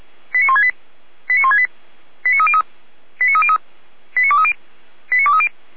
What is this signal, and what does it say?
Five-tone selective call (ZVEI), heard as quick runs of five tones, tones of about 70 ms. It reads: 89168 89168 89303 89303 89230 89230
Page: /455/SDR/ZVEI/89167/